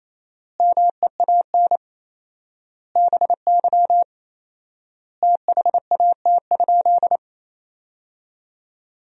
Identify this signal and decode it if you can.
Morse code, heard as a keyed tone, about 28 wpm, 700 Hz. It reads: MEAN BY THAT?
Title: 4x Word-Spacing - Sentences from Top 200 Words - 28wpm